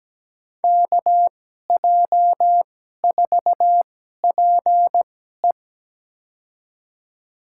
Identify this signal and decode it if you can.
Morse code, heard as a keyed tone, about 17 wpm, spaced as for 12 wpm, 700 Hz. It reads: KJ4PE
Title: USA Call Signs - 5 Characters - 2x Character Spacing - 17wpm